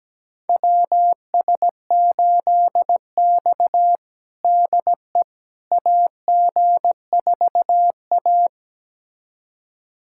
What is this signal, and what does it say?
Morse code, heard as a keyed tone, about 17 wpm, 700 Hz. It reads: WS8X DE AG4A
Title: USA Callsign DE Callsign - 4 Characters - 17wpm